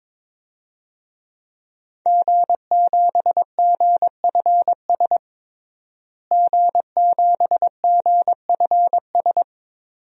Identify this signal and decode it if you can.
Morse code, heard as a keyed tone, about 22 wpm, 700 Hz. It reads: G7GFS G7GFS
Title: DXCC-100 Call Signs - 5 Characters - Repeated 2x - 22wpm